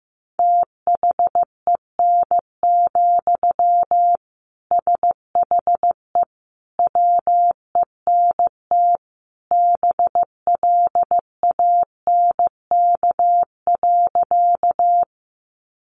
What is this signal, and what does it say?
Morse code, heard as a keyed tone, about 15 wpm, 700 Hz. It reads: THEN, SHE WENT BLANK.
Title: Sentences from Top 1500 Words 15wpm